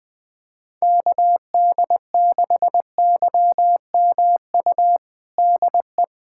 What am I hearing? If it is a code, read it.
Morse code, heard as a keyed tone, about 21 wpm, 700 Hz. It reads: KD6YMU DE
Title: USA Callsign DE Callsign - 6 Characters - 20wpm